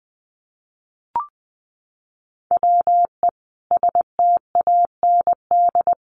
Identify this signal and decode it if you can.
Morse code, heard as a keyed tone, about 20 wpm, 700 Hz. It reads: WE STAND
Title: Sentences from Top 1200 Words 20wpm